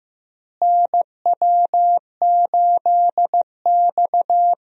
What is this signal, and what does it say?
Morse code, heard as a keyed tone, about 15 wpm, 700 Hz. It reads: NW8X
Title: USA Call Signs - 4 Characters - Repeated 3x - 15wpm